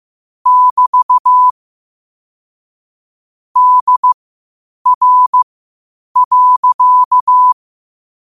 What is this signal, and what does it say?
Morse code, heard as a keyed tone, about 15 wpm, spaced as for 8 wpm, 1000 Hz. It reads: <BT> DR.